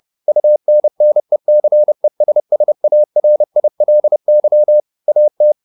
Morse code, heard as a keyed tone, about 30 wpm, 600 Hz. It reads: UNNECESSARILY AT
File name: the old man and the sea / the old man and the sea 1 30wpm